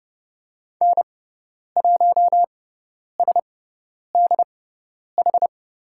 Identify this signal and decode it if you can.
Morse code, heard as a keyed tone, about 30 wpm, spaced as for 10 wpm, 700 Hz. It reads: N1SDH
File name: Lesson 17g - Letter u - Call Signs 30wpm